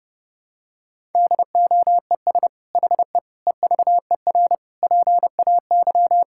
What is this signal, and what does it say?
Morse code, heard as a keyed tone, about 30 wpm, 700 Hz. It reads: DOES HE EVER PAY
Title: Sentences from Top 1000 Words - Review Part 5 - 30wpm